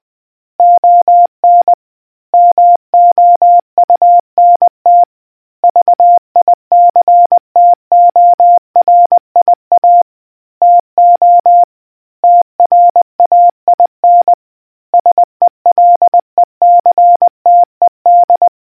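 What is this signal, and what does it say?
Morse code, heard as a keyed tone, about 20 wpm, 700 Hz. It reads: ON MOUNT VICTORIA TO TRAIN SELECTED
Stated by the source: Morse quiz question from the RNZN Communicators Association.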